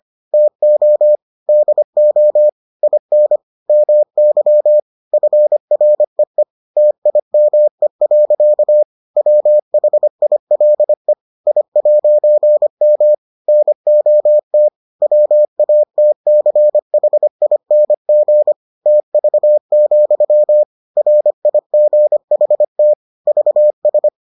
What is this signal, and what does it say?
Morse code, heard as a keyed tone, about 25 wpm, 600 Hz. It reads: TO DO IN MY FREE TIME. WHILE I'M NOT WATCHING TV, RIGHT VS